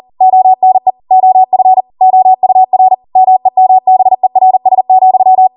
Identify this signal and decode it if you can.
Morse code, heard as a keyed tone, about 40 wpm, 750 Hz. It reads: ONE OF OUR MEMBERS,